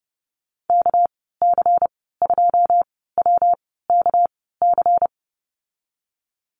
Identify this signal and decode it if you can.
Morse code, heard as a keyed tone, about 30 wpm, spaced as for 17 wpm, 700 Hz. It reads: KC2WKC
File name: Lesson 37g - Letter x - Call Signs 30wpm